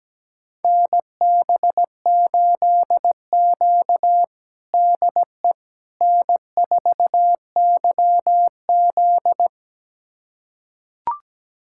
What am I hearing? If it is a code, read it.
Morse code, heard as a keyed tone, about 17 wpm, 700 Hz. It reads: NB8Q DE N4YZ
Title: USA Callsign DE Callsign - 4 Characters - 17wpm